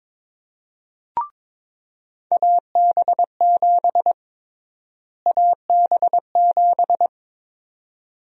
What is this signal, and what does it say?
Morse code, heard as a keyed tone, about 22 wpm, 700 Hz. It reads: AB7 AB7
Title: USA Call Sign Prefixes - Repeated 2x - 22wpm